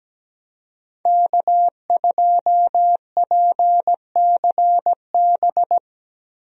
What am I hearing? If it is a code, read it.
Morse code, heard as a keyed tone, about 17 wpm, 700 Hz. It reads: K2PCB
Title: USA Call Signs - 5 Characters - Repeated 2x - 17wpm